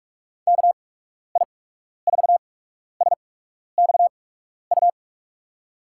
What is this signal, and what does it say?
Morse code, heard as a keyed tone, about 46 wpm, spaced as for 12 wpm, 700 Hz. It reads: KI4SXU
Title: USA Call Signs - 6 Characters - 8x Character Spacing - 45wpm